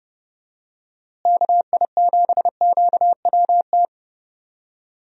Morse code, heard as a keyed tone, about 30 wpm, 700 Hz. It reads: KI7QWT
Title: USA Call Signs - 6 Characters - Repeated 2x - 30wpm